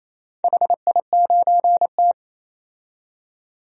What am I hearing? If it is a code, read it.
Morse code, heard as a keyed tone, about 27 wpm, 690 Hz. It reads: HI9T
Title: DXCC-100 Call Signs - 4 Characters - 28wpm